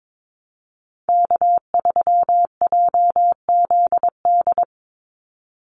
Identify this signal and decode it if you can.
Morse code, heard as a keyed tone, about 22 wpm, 700 Hz. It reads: K3JZD
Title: NA SOTA Chaser Callsigns 22wpm